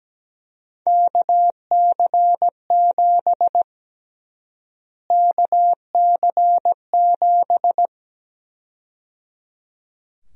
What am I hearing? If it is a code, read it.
Morse code, heard as a keyed tone, about 17 wpm, 700 Hz. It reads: KC7 KC7
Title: USA Call Sign Prefixes - Repeated 2x - 17wpm